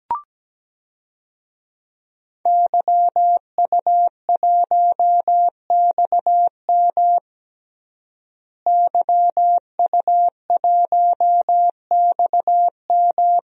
Morse code, heard as a keyed tone, about 17 wpm, 700 Hz. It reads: YU1XM YU1XM
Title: DXCC-100 Call Signs - 5 Characters - Repeated 3x - 17wpm